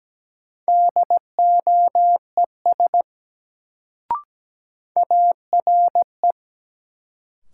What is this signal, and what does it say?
Morse code, heard as a keyed tone, about 17 wpm, 700 Hz. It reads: DOES ARE
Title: Sets of 2 Words - New Word Drill - 17wpm